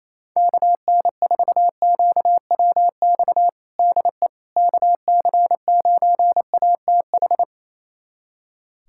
Morse code, heard as a keyed tone, about 28 wpm, 700 Hz. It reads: KN4QWX DE KC9ATH
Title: USA Callsign DE Callsign - 6 Characters - 28wpm